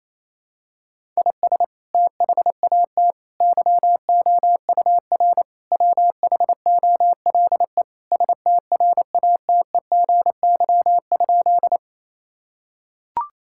Morse code, heard as a keyed tone, about 28 wpm, 700 Hz. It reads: IS THAT YOUR WHOLE STRATEGY?
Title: Sentences from Top 1000 Words - Review Part 5 - 28wpm